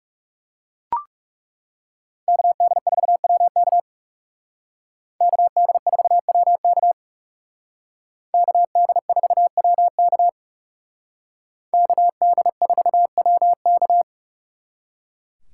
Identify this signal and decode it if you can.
Morse code, heard as a keyed tone, about 36 wpm, 700 Hz. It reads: KD4WK KD4WK KD4WK KD4WK
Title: Speed Racing - USA 2x2 Call Signs 30wpm